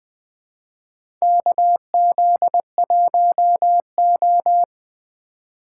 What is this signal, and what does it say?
Morse code, heard as a keyed tone, about 20 wpm, 700 Hz. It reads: KZ1O
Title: USA Call Signs - 4 Characters - 8x Character Spacing - 20wpm